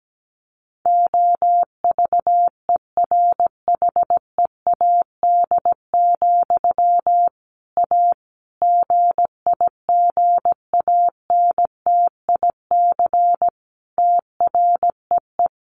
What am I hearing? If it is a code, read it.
Morse code, heard as a keyed tone, about 17 wpm, 700 Hz. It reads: OVERHEAD, A GIGANTIC TREE
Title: A Sound Of Thunder 17wpm